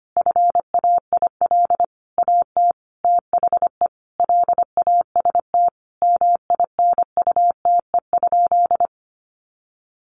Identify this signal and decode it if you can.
Morse code, heard as a keyed tone, about 25 wpm, 700 Hz. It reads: FAIL AT THE LAST MINUTE?
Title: Sentences from Top 1100 Words 25wpm